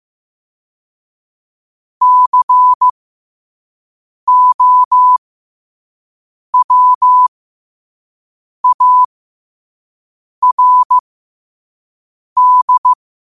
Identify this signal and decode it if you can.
Morse code, heard as a keyed tone, about 15 wpm, spaced as for 5 wpm, 1000 Hz. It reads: COWARD
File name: qotdfiles / 5wpm